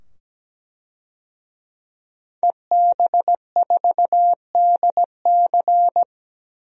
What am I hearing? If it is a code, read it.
Morse code, heard as a keyed tone, about 17 wpm, 700 Hz. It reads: EB4DC
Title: DXCC-100 Call Signs - 5 Characters - 17wpm